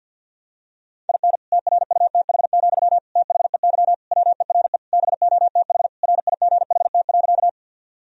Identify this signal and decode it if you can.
Morse code, heard as a keyed tone, about 48 wpm, 700 Hz. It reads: IN TRUTH, THEY WERE BOTH RIGHT.
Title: Sentences from Top 1000 Words - Review Part 1 - 50wpm